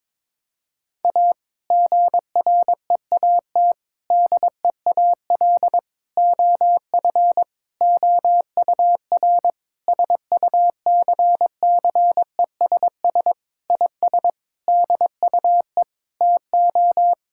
Morse code, heard as a keyed tone, about 22 wpm, 700 Hz. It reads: A GREAT DEAL OF OUR SUCCESS IS DUE TO
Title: Sentences from Top 1000 Words - Review Part 6 - 22wpm